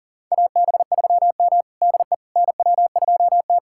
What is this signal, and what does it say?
Morse code, heard as a keyed tone, about 43 wpm, 690 Hz. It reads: AB3M DE NW2T
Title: USA Callsign DE Callsign - 4 Characters - 40wpm